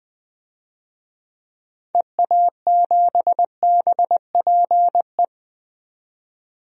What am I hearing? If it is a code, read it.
Morse code, heard as a keyed tone, about 20 wpm, 700 Hz. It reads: EA7BPE
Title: DXCC-100 Call Signs - 6 Characters - 20wpm